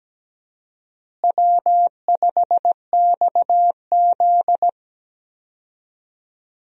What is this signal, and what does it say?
Morse code, heard as a keyed tone, about 17 wpm, 700 Hz. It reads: W5XZ
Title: USA Call Signs - 4 Characters - Repeated 2x - 17wpm